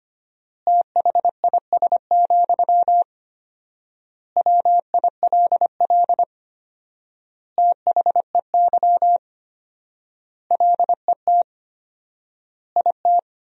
Morse code, heard as a keyed tone, about 25 wpm, 700 Hz. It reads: THIS, WILL THEY LET IT
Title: 4x Word-Spacing - Sentences from Top 400 Words - 25wpm